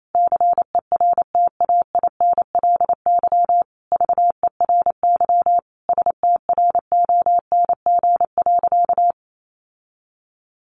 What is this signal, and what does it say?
Morse code, heard as a keyed tone, about 28 wpm, 700 Hz. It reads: CERTAINLY VERY STRONG.
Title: Sentences from Top 1300 Words 28wpm